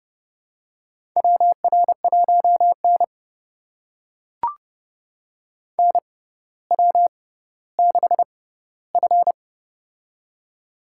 Morse code, heard as a keyed tone, about 30 wpm, spaced as for 15 wpm, 700 Hz. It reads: WR1N NW6F
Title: USA Call Signs - 4 Characters - 6x Character Spacing - 30wpm